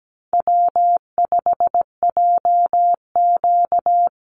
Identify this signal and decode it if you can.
Morse code, heard as a keyed tone, about 17 wpm, 700 Hz. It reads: W5JQ
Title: Easy Call Signs - 17wpm